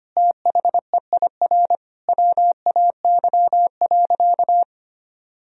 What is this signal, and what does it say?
Morse code, heard as a keyed tone, about 25 wpm, 700 Hz. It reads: THEIR WAY.
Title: Sentences from Top 1000 Words - Review Part 4 - 25wpm